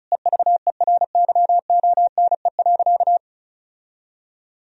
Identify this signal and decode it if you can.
Morse code, heard as a keyed tone, about 34 wpm, 700 Hz. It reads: EVERYONE.
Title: Sentences from Top 1000 Words - Review Part 4 - 35wpm